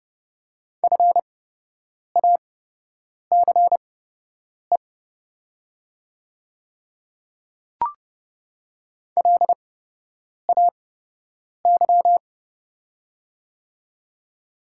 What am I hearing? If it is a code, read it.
Morse code, heard as a keyed tone, about 30 wpm, spaced as for 8 wpm, 700 Hz. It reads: FACE LAY
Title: 8x Character Spacing - Top 500 Words - 30wpm